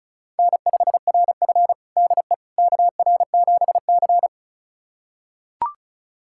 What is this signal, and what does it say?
Morse code, heard as a keyed tone, about 34 wpm, 700 Hz. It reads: N5RF DE KR7C
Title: USA Callsign DE Callsign - 4 Characters - 35wpm